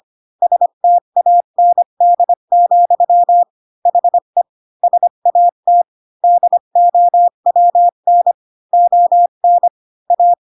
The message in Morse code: STAND, HE SAT DOWN ON A

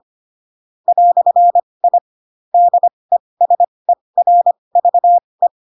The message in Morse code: "I DESERVE

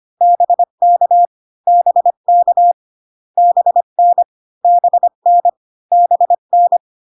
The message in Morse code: BK BK BN BN BN

